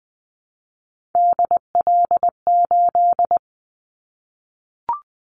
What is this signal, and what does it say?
Morse code, heard as a keyed tone, about 20 wpm, 700 Hz. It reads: DL8
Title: Callsign Prefixes for Top 25 DXCC - 20wpm